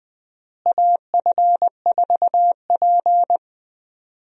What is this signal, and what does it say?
Morse code, heard as a keyed tone, about 21 wpm, 690 Hz. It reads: AF4P
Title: USA Call Signs - 4 Characters - Repeated 3x - 20wpm